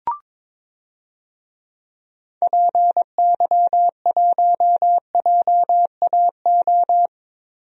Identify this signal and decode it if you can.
Morse code, heard as a keyed tone, about 22 wpm, 700 Hz. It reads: PY1JAO
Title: DXCC-100 Call Signs - 6 Characters - Repeated 2x - 22wpm